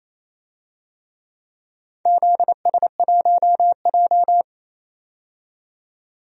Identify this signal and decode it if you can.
Morse code, heard as a keyed tone, about 28 wpm, 700 Hz. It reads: ZS1J
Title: DXCC-100 Call Signs - 4 Characters - 28wpm